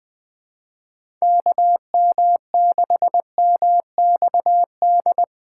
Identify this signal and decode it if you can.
Morse code, heard as a keyed tone, about 21 wpm, 700 Hz. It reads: KM6MXD
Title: USA Call Signs - 6 Characters - 8x Character Spacing - 20wpm